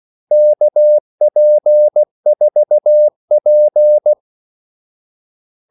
Morse code, heard as a keyed tone, about 16 wpm, 600 Hz. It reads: KP4P
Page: Kaldesignaler 16 wpm